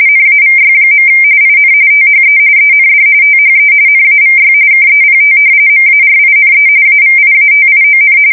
RTTY, about 45.45 baud. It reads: NAVY. IN THE END THE NAVY KEPT MARINE AVIATION
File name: fsk170 / usnavy5